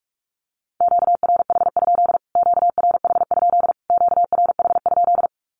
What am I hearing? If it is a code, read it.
Morse code, heard as a keyed tone, about 44 wpm, 700 Hz. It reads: QRH? QRH? QRH?
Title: Q-Codes Repeated 3x 45wpm